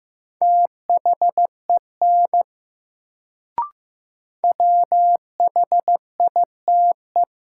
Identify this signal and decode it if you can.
Morse code, heard as a keyed tone, about 15 wpm, 700 Hz. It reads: THEN WHITE
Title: Sets of 2 Words - New Word Drill - 15wpm